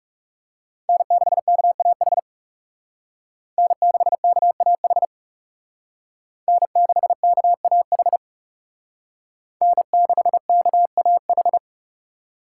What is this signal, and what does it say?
Morse code, heard as a keyed tone, about 38 wpm, 700 Hz. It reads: N6KAH N6KAH N6KAH N6KAH